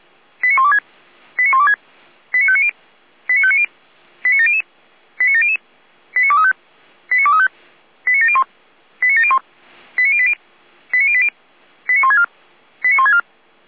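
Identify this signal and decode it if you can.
Five-tone selective call (ZVEI), heard as quick runs of five tones, tones of about 70 ms. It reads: 89126 89126 89590 89590 89700 89700 89235 89235 89701 89701 89080 89080 89164 89164
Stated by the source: ZVEI five-tone selective call 89115 received by SDR.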